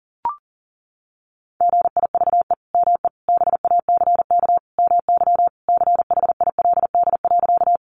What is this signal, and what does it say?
Morse code, heard as a keyed tone, about 40 wpm, 700 Hz. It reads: GIVE ME BACK MY CHILD.
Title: Sentences from Top 800 Words 40wpm